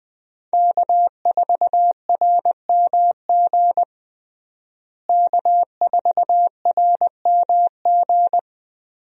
Morse code, heard as a keyed tone, about 20 wpm, 700 Hz. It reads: K4RMG K4RMG
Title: USA Call Signs - 5 Characters - Repeated 3x - 20wpm